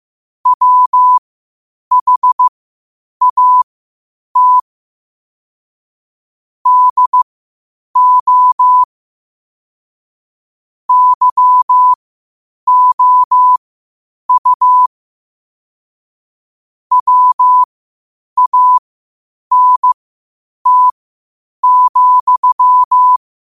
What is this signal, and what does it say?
Morse code, heard as a keyed tone, about 15 wpm, spaced as for 8 wpm, 1000 Hz. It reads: WHAT DO YOU WANT,
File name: qotdfiles / 8wpm